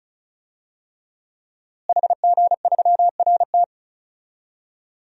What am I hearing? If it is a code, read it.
Morse code, heard as a keyed tone, about 36 wpm, 700 Hz. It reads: HG3RT
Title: DXCC-100 Call Signs - 5 Characters - Repeated 3x - 35wpm